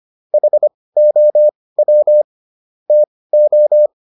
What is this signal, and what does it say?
Morse code, heard as a keyed tone, about 25 wpm, spaced as for 18 wpm, 600 Hz. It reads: HOW TO
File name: SS108 18